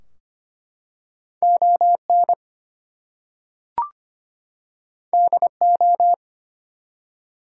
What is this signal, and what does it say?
Morse code, heard as a keyed tone, about 25 wpm, 700 Hz. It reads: ON DO
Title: Two Letter Words - 25wpm